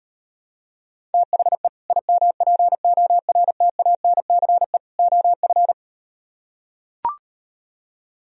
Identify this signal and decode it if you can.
Morse code, heard as a keyed tone, about 38 wpm, 700 Hz. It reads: THE IMPORTANCE OF